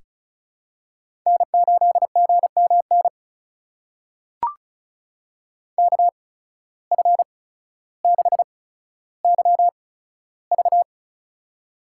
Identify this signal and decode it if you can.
Morse code, heard as a keyed tone, about 36 wpm, spaced as for 15 wpm, 700 Hz. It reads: N8GMN KF6YV
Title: USA Call Signs - 5 Characters - 8x Character Spacing - 35wpm